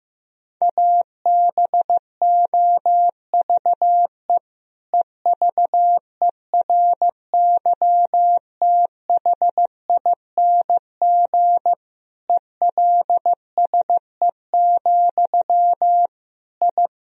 Morse code, heard as a keyed tone, about 15 wpm, 700 Hz. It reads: ABOVE EVERYTHING ELSE, I
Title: Sentences from Top 1000 Words - Review Part 2 - 15wpm